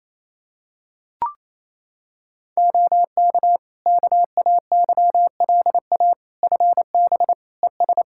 Morse code, heard as a keyed tone, about 28 wpm, 700 Hz. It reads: OK KAYLA FB ES
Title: K1USN SST QSO Practice 28wpm